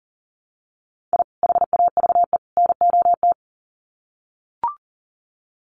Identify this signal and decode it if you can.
Morse code, heard as a keyed tone, about 40 wpm, 700 Hz. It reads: I HAVE NOT
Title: Sets of 3 Words 40wpm